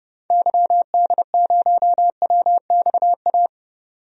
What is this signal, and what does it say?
Morse code, heard as a keyed tone, about 30 wpm, 700 Hz. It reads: YD0WXA
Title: DXCC-100 Call Signs - 6 Characters - 30wpm